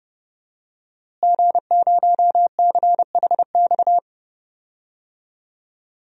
Morse code, heard as a keyed tone, about 32 wpm, 700 Hz. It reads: G0CHX
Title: DXCC-100 Call Signs - 5 Characters - Repeated 2x - 30wpm